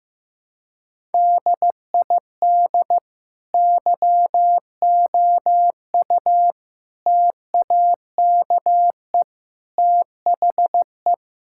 Morse code, heard as a keyed tone, about 15 wpm, 700 Hz. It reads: DID YOU TAKE THE
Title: Sentences from Top 1000 Words - Review Part 6 - 15wpm